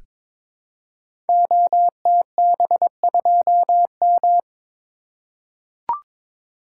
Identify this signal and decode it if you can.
Morse code, heard as a keyed tone, about 22 wpm, 700 Hz. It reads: OTB2M
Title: Code Groups - 5 Letter:Numbers - 22wpm